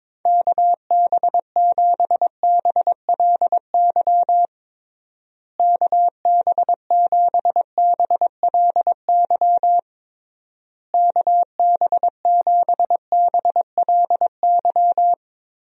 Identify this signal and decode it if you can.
Morse code, heard as a keyed tone, about 22 wpm, 700 Hz. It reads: KB7BLY KB7BLY KB7BLY
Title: USA Call Signs - 6 Characters - Repeated 3x - 22wpm